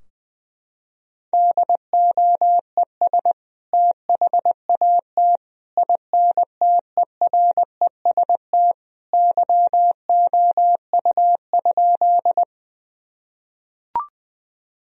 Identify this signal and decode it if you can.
Morse code, heard as a keyed tone, about 20 wpm, 700 Hz. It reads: DOES THAT INTEREST YOU?
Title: Sentences from Top 1000 Words - Review Part 2 - 20wpm